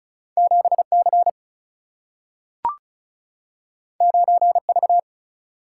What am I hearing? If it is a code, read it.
Morse code, heard as a keyed tone, about 36 wpm, 700 Hz. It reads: 7 C 9 V
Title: Single-Letter Single-Number 35wpm